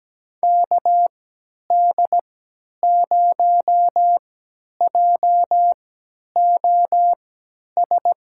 Morse code, heard as a keyed tone, about 17 wpm, spaced as for 10 wpm, 700 Hz. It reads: KD0JOS
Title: USA Call Signs - 6 Characters - 3x Character Spacing - 17wpm